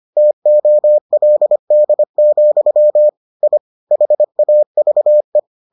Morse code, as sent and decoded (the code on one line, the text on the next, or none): - --- .-.. -.. --..-- / .. / .... .- ...- .
TOLD, I HAVE